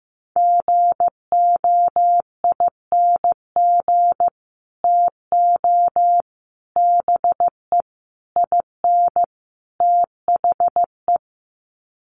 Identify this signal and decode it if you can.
Morse code, heard as a keyed tone, about 15 wpm, 700 Hz. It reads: GOING TO BE IN THE
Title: Sets of 5 Words Encore 15wpm